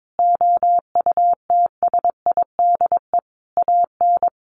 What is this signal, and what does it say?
Morse code, heard as a keyed tone, about 22 wpm, 700 Hz. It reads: OUTSIDE AN
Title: Sentences from Top 1100 Words 22wpm